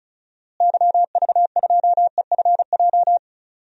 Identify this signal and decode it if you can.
Morse code, heard as a keyed tone, about 36 wpm, 700 Hz. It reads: YV2EFJ
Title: DXCC-100 Call Signs - 6 Characters - Repeated 2x - 35wpm